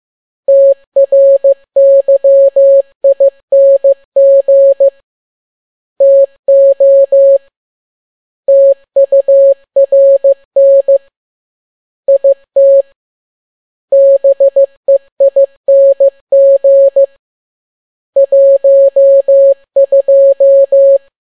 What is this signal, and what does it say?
Morse code, heard as a keyed tone, about 15 wpm, 550 Hz. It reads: TRYING TO TURN IT BEING 12
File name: n6tt p1 15C13